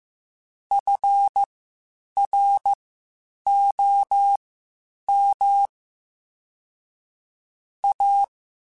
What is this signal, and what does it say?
Morse code, heard as a keyed tone, about 15 wpm, spaced as for 8 wpm, 800 Hz. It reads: FROM A
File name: (spaced) / QSOs Spaced-mp3s / WA7QOX 2 @13 sp